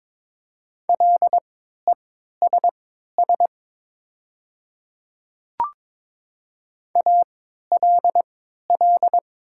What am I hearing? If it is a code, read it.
Morse code, heard as a keyed tone, about 22 wpm, spaced as for 13 wpm, 700 Hz. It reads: LESS ALL
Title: 3x Character Spacing - Top 500 Words - 22wpm